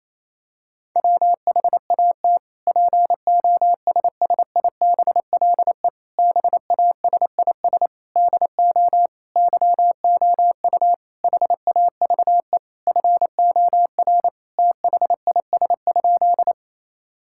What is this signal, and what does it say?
Morse code, heard as a keyed tone, about 28 wpm, 700 Hz. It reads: WHAT POSSIBLE BASIS DO YOU HAVE FOR THIS?
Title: Sentences from Top 1000 Words - Review Part 5 - 28wpm